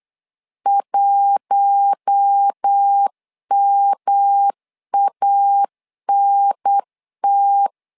Morse code, heard as a keyed tone, about 8 wpm, 800 Hz. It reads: 1MANT
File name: LEZIONI / 4 Q1 8wpm 3035